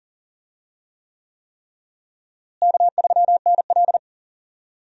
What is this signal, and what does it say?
Morse code, heard as a keyed tone, about 38 wpm, 700 Hz. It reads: K3NL
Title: USA Call Signs - 4 Characters - Repeated 3x - 40wpm